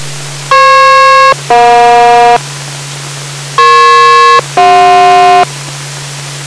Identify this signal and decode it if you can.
Aircraft SELCAL, heard as two pairs of tones. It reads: FP-EJ EP-CJ